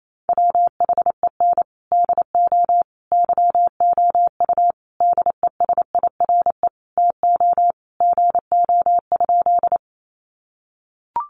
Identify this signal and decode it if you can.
Morse code, heard as a keyed tone, about 28 wpm, 700 Hz. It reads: WHEN DO YOU DESIRE TO GO?
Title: Sentences from Top 1100 Words 28wpm